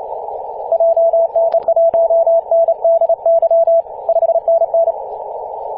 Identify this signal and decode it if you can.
Morse code, heard as a keyed tone, about 29 wpm, 670 Hz. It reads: JN1NDY 5NN
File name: ep2a-21-029mhz-cw